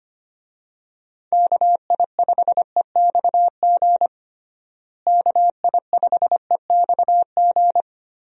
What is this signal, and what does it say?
Morse code, heard as a keyed tone, about 25 wpm, 700 Hz. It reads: KI5EXG KI5EXG
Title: USA Call Signs - 6 Characters - Repeated 2x - 25wpm